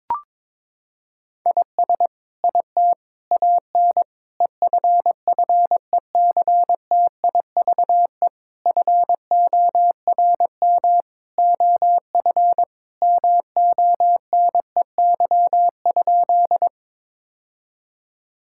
Morse code, heard as a keyed tone, about 23 wpm, 700 Hz. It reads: IS IT AN EFFECTIVE FORM OF MONEY?
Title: Sentences from Top 1000 Words - Review Part 6 - 22wpm